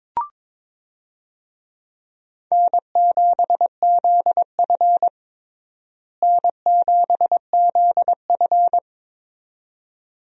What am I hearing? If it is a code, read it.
Morse code, heard as a keyed tone, about 22 wpm, 700 Hz. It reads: N7ZF N7ZF
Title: USA Call Signs - 4 Characters - Repeated 2x - 22wpm